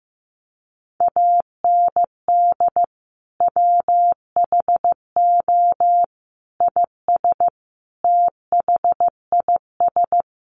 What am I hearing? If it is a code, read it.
Morse code, heard as a keyed tone, about 15 wpm, 700 Hz. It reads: AND WHO IS THIS
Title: Sentences from Top 1100 Words 15wpm